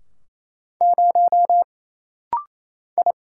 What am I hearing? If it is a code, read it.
Morse code, heard as a keyed tone, about 27 wpm, 700 Hz. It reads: 0 I
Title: Single Letter-Number - Rapid-Fire 28wpm